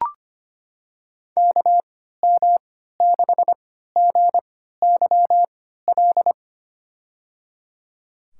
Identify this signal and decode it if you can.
Morse code, heard as a keyed tone, about 25 wpm, spaced as for 14 wpm, 700 Hz. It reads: KM6GYL
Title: USA Call Signs - 3x Character Spacing - 25wpm